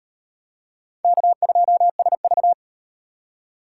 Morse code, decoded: K2SV

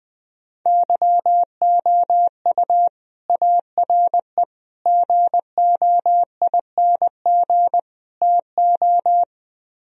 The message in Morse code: YOU ARE GOING TO

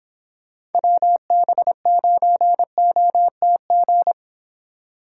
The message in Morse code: WB9OTG

